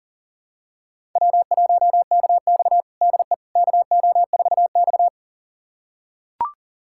Morse code, heard as a keyed tone, about 40 wpm, 700 Hz. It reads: W1KX DE KO4X
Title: USA Callsign DE Callsign - 4 Characters - 40wpm